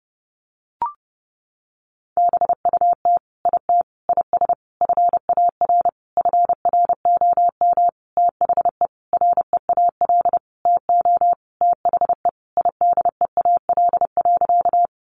Morse code, heard as a keyed tone, about 30 wpm, 700 Hz. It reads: BUT IT IS FAR FROM THE REAL TO THE IDEAL.
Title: Sentences from Top 1400 Words 30wpm